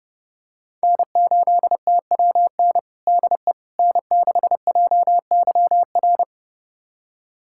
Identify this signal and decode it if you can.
Morse code, heard as a keyed tone, about 30 wpm, 700 Hz. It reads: N8TWN DE N6JYR
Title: USA Callsign DE Callsign 30wpm